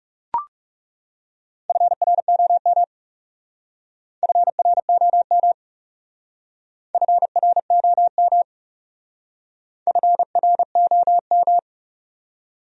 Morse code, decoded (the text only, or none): FROM FROM FROM FROM